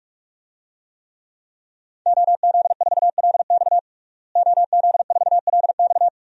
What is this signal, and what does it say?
Morse code, heard as a keyed tone, about 46 wpm, 700 Hz. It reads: OZ4LX OZ4LX
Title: DXCC-100 Call Signs - 5 Characters - Repeated 2x - 45wpm